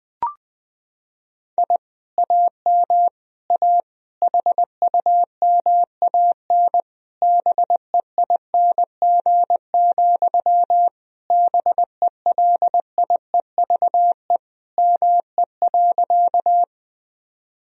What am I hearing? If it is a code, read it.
Morse code, heard as a keyed tone, about 20 wpm, 700 Hz. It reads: I AM A HUMAN BEING, BELIEVE ME.
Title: Sentences from Top 1000 Words - Review Part 3 - 20wpm